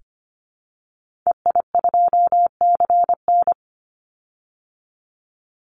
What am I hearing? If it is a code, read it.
Morse code, heard as a keyed tone, about 25 wpm, 700 Hz. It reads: EI2CN
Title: Call Signs Repeated 3x - 25wpm